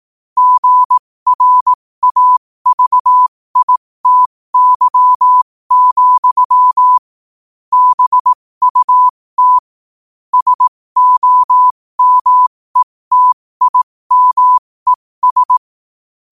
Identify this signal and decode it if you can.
Morse code, heard as a keyed tone, about 18 wpm, spaced as for 15 wpm, 1000 Hz. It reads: GRAVITY, BUT SOMETIMES
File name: 15wpm